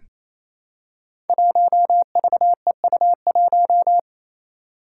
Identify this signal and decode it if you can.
Morse code, heard as a keyed tone, about 28 wpm, 700 Hz. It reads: 1VEU1
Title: Five Challenging Letter-Number Combinations 28wpm